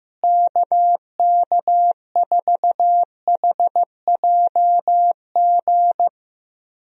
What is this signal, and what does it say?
Morse code, heard as a keyed tone, about 16 wpm, 700 Hz. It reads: KK4HJG
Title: USA Call Signs - 6 Characters - Repeated 3x - 15wpm